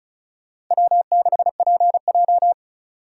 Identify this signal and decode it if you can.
Morse code, heard as a keyed tone, about 36 wpm, 700 Hz. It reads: W6PJ
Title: Speed Racing - USA 1x2 Call Signs 30wpm